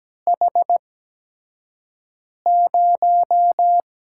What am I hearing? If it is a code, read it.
Morse code, heard as a keyed tone, about 17 wpm, spaced as for 5 wpm, 700 Hz. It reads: H0
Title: USA Call Signs - 4 Characters - 8x Character Spacing - 17wpm